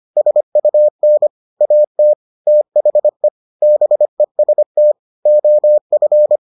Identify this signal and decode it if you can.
Morse code, heard as a keyed tone, about 25 wpm, 600 Hz. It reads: SUN AT THE BEST OF